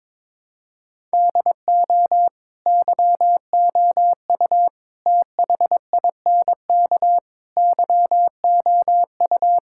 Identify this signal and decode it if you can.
Morse code, heard as a keyed tone, about 22 wpm, 700 Hz. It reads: DO YOU THINK YOU
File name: practice / 22wpm / Sentences with 4-Word Phrases - 22wpm